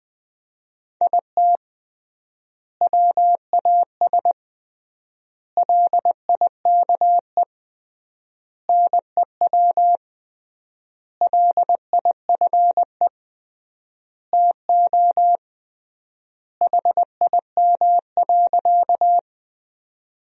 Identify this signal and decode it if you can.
Morse code, heard as a keyed tone, about 20 wpm, 700 Hz. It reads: IT WAS LIKE NEW LIFE TO HIM.
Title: 3x Word-Spacing - Sentences from Top 300 Words - 20wpm